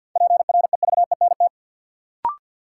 Morse code, heard as a keyed tone, about 50 wpm, 700 Hz. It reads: PREVENT
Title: Sentences from Top 1100 Words - New Word Drill Alt - 50wpm